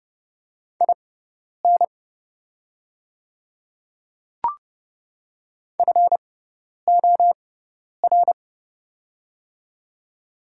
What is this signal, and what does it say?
Morse code, heard as a keyed tone, about 30 wpm, spaced as for 10 wpm, 700 Hz. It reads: IN FOR
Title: 6x Character Spacing - Top 200 Words - 30wpm